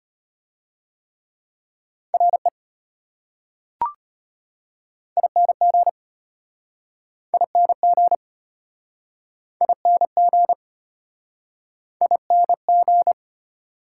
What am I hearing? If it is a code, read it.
Morse code, heard as a keyed tone, about 33 wpm, 690 Hz. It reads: RE ING ING ING ING